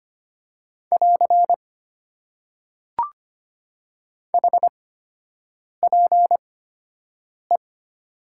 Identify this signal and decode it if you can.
Morse code, heard as a keyed tone, about 25 wpm, spaced as for 7 wpm, 700 Hz. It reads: <AR> HPE
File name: Top 90 QSO Elements - 8x Character Spacing - 25wpm